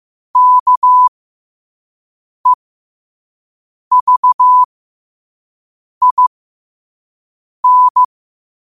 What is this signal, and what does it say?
Morse code, heard as a keyed tone, about 15 wpm, spaced as for 5 wpm, 1000 Hz. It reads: KEVIN
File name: qotdfiles / 5wpm